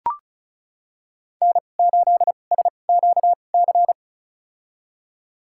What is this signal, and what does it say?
Morse code, heard as a keyed tone, about 34 wpm, spaced as for 25 wpm, 700 Hz. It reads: N8SQC
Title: USA Call Signs - 2x Character Spacing - 35wpm